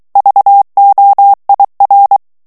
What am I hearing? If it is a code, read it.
Morse code, heard as a keyed tone, about 23 wpm, 800 Hz. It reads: VOIR
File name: FRANCAIS TEXTE CLAIR 2 VITESSE 22 MOTS MINUTES